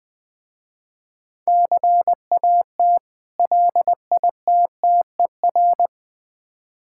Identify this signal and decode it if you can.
Morse code, heard as a keyed tone, about 21 wpm, 700 Hz. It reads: CAT LITTER
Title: Difficult Words with Hints - 20wpm